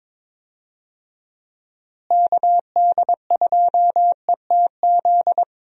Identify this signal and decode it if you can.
Morse code, heard as a keyed tone, about 22 wpm, 700 Hz. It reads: KD2ETZ
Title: USA Call Signs - 6 Characters - Repeated 3x - 22wpm